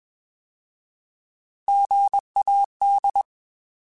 Morse code, heard as a keyed tone, about 20 wpm, 800 Hz. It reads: GAD
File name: Alphabetical / Alphabetical mp3s / Gs @20